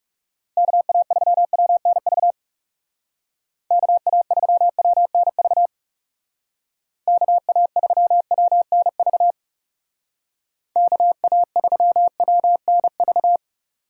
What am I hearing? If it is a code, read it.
Morse code, heard as a keyed tone, about 36 wpm, 700 Hz. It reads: KA3WNV KA3WNV KA3WNV KA3WNV
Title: Speed Racing - USA 2x3 Call Signs 30wpm